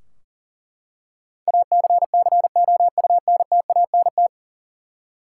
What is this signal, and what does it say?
Morse code, heard as a keyed tone, about 37 wpm, 700 Hz. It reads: ACCOUNTANT
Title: Practice QSO - Exchange 3 - More Info - 40wpm